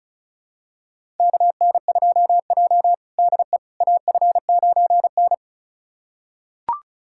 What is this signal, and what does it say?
Morse code, heard as a keyed tone, about 36 wpm, 700 Hz. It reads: KN2J DE AF9N
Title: USA Callsign DE Callsign - 4 Characters - 35wpm